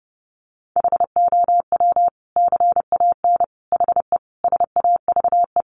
Morse code, heard as a keyed tone, about 30 wpm, 700 Hz. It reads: HOW CAN HE SAVE
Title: Sentences from Top 1100 Words 30wpm